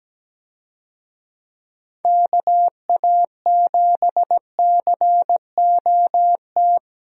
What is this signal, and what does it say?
Morse code, heard as a keyed tone, about 17 wpm, 700 Hz. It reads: KA7COT
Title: USA Call Signs - 6 Characters - Repeated 2x - 17wpm